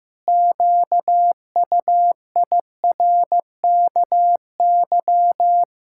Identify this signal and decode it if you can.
Morse code, heard as a keyed tone, about 15 wpm, 700 Hz. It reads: QUIRKY
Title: Pangram Challenge v2 - 15wpm